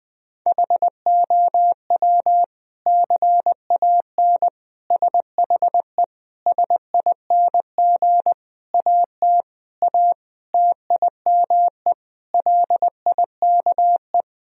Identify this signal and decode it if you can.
Morse code, heard as a keyed tone, about 20 wpm, 700 Hz. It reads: HOW CAN SHE SING AT A TIME LIKE
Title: Sentences from Top 1000 Words - Review Part 2 - 20wpm